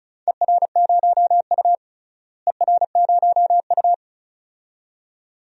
Morse code, heard as a keyed tone, about 34 wpm, 700 Hz. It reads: ER0U ER0U
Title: DXCC-100 Call Signs - 4 Characters - Repeated 2x - 35wpm